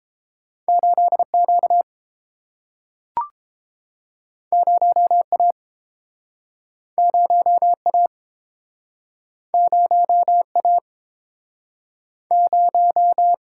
Morse code, heard as a keyed tone, about 31 wpm, 700 Hz. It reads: 8Q 0A 0A 0A 0